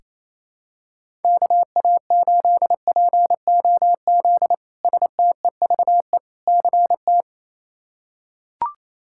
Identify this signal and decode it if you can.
Morse code, heard as a keyed tone, about 28 wpm, 700 Hz. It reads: KA8POZ STEVE CT
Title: K1USN SST QSO Practice - 28wpm